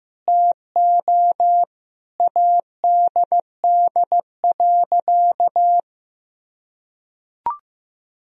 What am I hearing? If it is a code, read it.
Morse code, heard as a keyed tone, about 15 wpm, 700 Hz. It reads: TO ADD.
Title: Sentences from Top 1000 Words - Review Part 1 - 15wpm